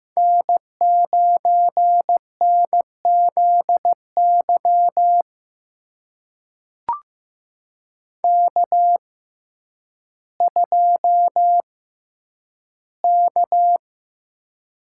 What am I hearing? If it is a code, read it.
Morse code, heard as a keyed tone, about 15 wpm, spaced as for 9 wpm, 700 Hz. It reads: N9NZY K2K
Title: USA Call Signs - 5 Characters - 6x Character Spacing - 15wpm